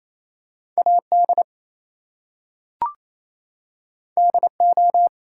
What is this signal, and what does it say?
Morse code, heard as a keyed tone, about 29 wpm, 700 Hz. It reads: AD DO
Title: Two Letter Words - 28wpm